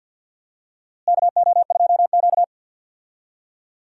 Morse code, heard as a keyed tone, about 51 wpm, 700 Hz. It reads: KO2Q
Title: USA Call Signs - 4 Characters - Repeated 2x - 50wpm